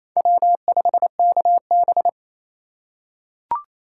Morse code, heard as a keyed tone, about 28 wpm, 700 Hz. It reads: W5KB
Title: USA Call Signs - 4x Character Spacing - 28wpm